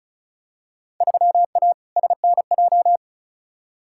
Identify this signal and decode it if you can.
Morse code, heard as a keyed tone, about 36 wpm, 700 Hz. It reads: 3A SNJ
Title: Field Day Exchanges - 35wpm